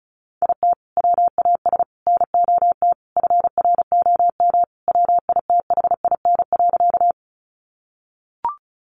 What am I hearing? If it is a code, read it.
Morse code, heard as a keyed tone, about 34 wpm, 700 Hz. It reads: IT WAS NOT FROM WITHIN.
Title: Sentences from Top 700 Words 35wpm